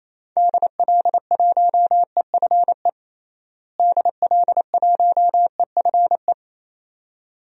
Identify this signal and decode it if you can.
Morse code, heard as a keyed tone, about 28 wpm, 700 Hz. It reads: DL1EFE DL1EFE
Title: DXCC-100 Call Signs - 6 Characters - Repeated 2x - 28wpm